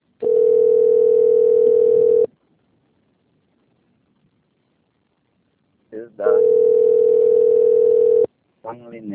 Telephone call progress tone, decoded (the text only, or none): ringback tone